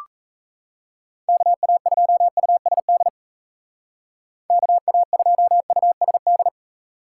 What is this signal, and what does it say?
Morse code, heard as a keyed tone, about 37 wpm, 700 Hz. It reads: KA2USD KA2USD